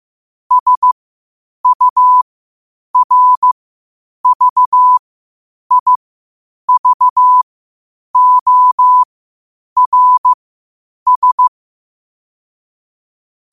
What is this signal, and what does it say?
Morse code, heard as a keyed tone, about 15 wpm, spaced as for 8 wpm, 1000 Hz. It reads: SURVIVORS
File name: qotdfiles / 8wpm